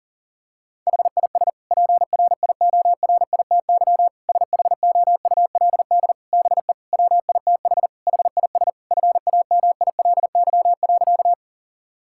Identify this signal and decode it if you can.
Morse code, heard as a keyed tone, about 40 wpm, 700 Hz. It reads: HIS PRIORITY SHOULD BE WITH HIS FAMILY.
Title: Sentences from Top 1600 Words 40wpm